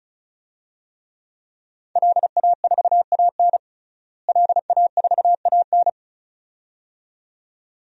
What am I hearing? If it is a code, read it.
Morse code, heard as a keyed tone, about 36 wpm, 700 Hz. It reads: LA4AN LA4AN
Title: DXCC-100 Call Signs - 5 Characters - Repeated 2x - 35wpm